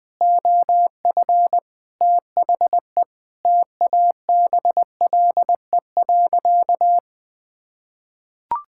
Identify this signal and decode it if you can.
Morse code, heard as a keyed tone, about 20 wpm, 700 Hz. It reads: OF THE TABLE.
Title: Sentences with 5-Word Phrases - Spoken Beginning - 20wpm